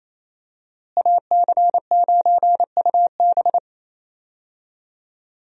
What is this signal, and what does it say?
Morse code, heard as a keyed tone, about 28 wpm, 700 Hz. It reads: AC9UB
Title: USA Call Signs - 5 Characters - Repeated 2x - 28wpm